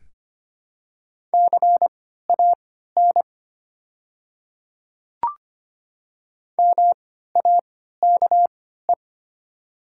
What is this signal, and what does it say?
Morse code, heard as a keyed tone, about 25 wpm, spaced as for 14 wpm, 700 Hz. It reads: CAN MAKE
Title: 3x Character Spacing - Top 100 Words - 25wpm